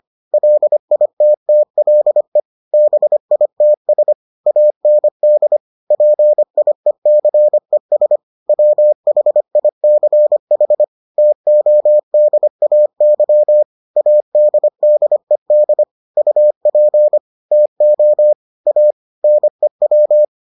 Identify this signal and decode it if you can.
Morse code, heard as a keyed tone, about 25 wpm, 600 Hz. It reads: LITTLE BITS AND PIECES WHICH TODAY ADDED UP TO A NEW